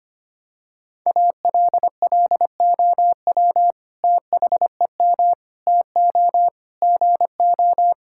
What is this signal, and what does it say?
Morse code, heard as a keyed tone, about 25 wpm, 700 Hz. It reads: ALLOW THEM TO GO
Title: Sentences from Top 1000 Words - Review Part 5 - 25wpm